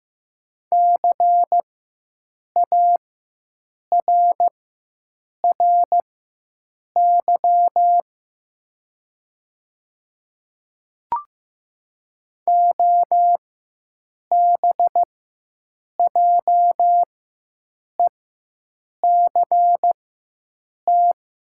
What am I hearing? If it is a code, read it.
Morse code, heard as a keyed tone, about 15 wpm, spaced as for 7 wpm, 700 Hz. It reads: CARRY OBJECT
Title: 4x Character Spacing - Top 500 Words - 15wpm